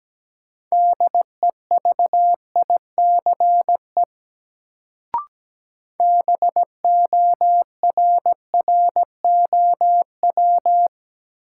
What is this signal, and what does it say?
Morse code, heard as a keyed tone, about 17 wpm, 700 Hz. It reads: DEVICE BORROW
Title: Sentences from Top 1500 Words - New Word Drill Alt - 17wpm